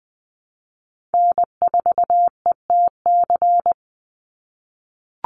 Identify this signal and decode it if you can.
Morse code, heard as a keyed tone, about 20 wpm, 700 Hz. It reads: N4ETC
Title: Call Signs - 20wpm